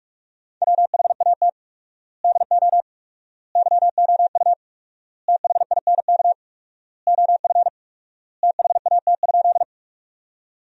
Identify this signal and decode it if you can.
Morse code, heard as a keyed tone, about 47 wpm, 700 Hz. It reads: WHAT DO YOU THINK OF THAT?
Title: 4x Word-Spacing - Sentences from Top 200 Words - 45wpm